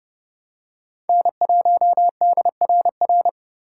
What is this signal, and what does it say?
Morse code, heard as a keyed tone, about 30 wpm, 700 Hz. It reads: N1DRR
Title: USA Call Signs - 3x Character Spacing - 30wpm